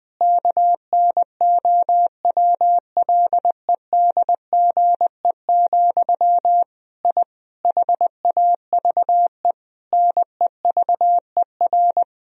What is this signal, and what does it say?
Morse code, heard as a keyed tone, about 20 wpm, 700 Hz. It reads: KNOWLEDGE, I HAVE NEVER
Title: Sentences from Top 1000 Words - Review Part 3 - 20wpm